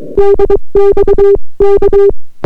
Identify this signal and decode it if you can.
Morse code, heard as a keyed tone, about 22 wpm, 400 Hz. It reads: DX K